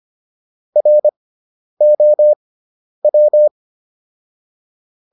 Morse code, heard as a keyed tone, about 25 wpm, spaced as for 10 wpm, 600 Hz. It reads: ROW